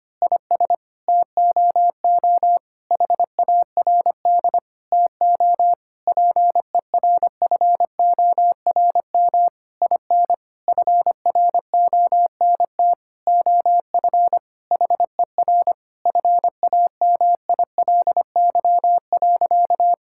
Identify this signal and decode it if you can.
Morse code, heard as a keyed tone, about 25 wpm, 700 Hz. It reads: IS TOO HARD TO PERFORM IN FRONT OF HER FAMILY.
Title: Sentences from Top 1000 Words - Review Part 5 - 25wpm